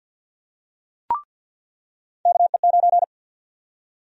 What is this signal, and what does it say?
Morse code, heard as a keyed tone, about 48 wpm, 700 Hz. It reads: KE9
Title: USA Call Sign Prefixes - 50wpm